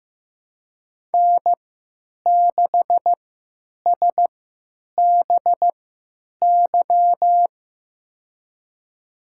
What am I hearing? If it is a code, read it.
Morse code, heard as a keyed tone, about 16 wpm, spaced as for 9 wpm, 700 Hz. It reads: N6SBY
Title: USA Call Signs - 5 Characters - 3x Character Spacing - 15wpm